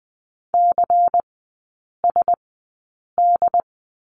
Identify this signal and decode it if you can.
Morse code, heard as a keyed tone, about 20 wpm, spaced as for 8 wpm, 700 Hz. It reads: CSD
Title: 114 Lesson 19e - Letter w - Three Characters 20wpm